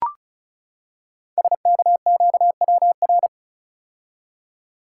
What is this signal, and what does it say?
Morse code, heard as a keyed tone, about 36 wpm, 690 Hz. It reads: SKQWR
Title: Code Groups - 5 Letter:Numbers - 35wpm